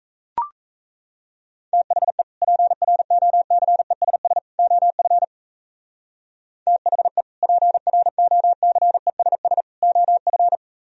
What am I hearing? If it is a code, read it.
Morse code, heard as a keyed tone, about 34 wpm, 700 Hz. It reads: THE PROCESS OF THE PROCESS OF
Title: Speed-Racing - Sets of 3 Words 28wpm